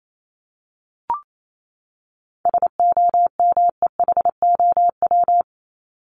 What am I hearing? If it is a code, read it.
Morse code, heard as a keyed tone, about 28 wpm, 700 Hz. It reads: SOMEHOW
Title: Top 1501-1750 Words 28wpm